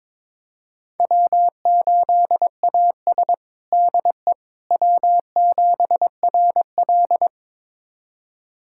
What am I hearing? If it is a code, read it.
Morse code, heard as a keyed tone, about 21 wpm, 700 Hz. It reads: W8AS DE W7RL
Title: USA Callsign DE Callsign - 4 Characters - 22wpm